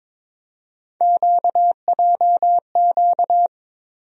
Morse code, heard as a keyed tone, about 22 wpm, 700 Hz. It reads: QJQ
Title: USA Call Sign Postfixes - 3 Letters 22wpm